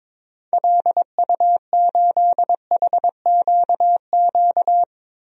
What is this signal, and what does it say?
Morse code, heard as a keyed tone, about 22 wpm, 700 Hz. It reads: LU8HQQ
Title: DXCC-100 Call Signs - 6 Characters - 22wpm